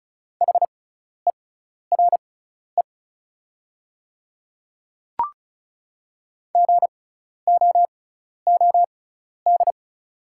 Morse code, heard as a keyed tone, about 36 wpm, spaced as for 12 wpm, 690 Hz. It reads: HERE GOOD
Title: 6x Character Spacing - Top 300 Words - 35wpm